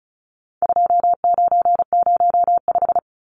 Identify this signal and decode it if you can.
Morse code, heard as a keyed tone, about 34 wpm, 700 Hz. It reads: 2905
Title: Numbers - 4-Digit 35wpm